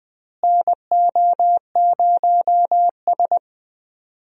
Morse code, heard as a keyed tone, about 21 wpm, 700 Hz. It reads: NO0S
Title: USA Call Signs - 4 Characters - Repeated 3x - 20wpm